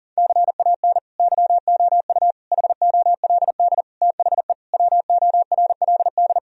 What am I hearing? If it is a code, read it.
Morse code, heard as a keyed tone, about 37 wpm, 700 Hz. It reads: CAN YOU HOLD THE WORLD
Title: Sentences from Top 1000 Words - Review Part 2 - 40wpm